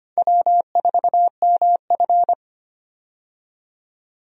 Morse code, heard as a keyed tone, about 25 wpm, 700 Hz. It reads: W4MF
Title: USA Call Signs - 4 Characters - 25wpm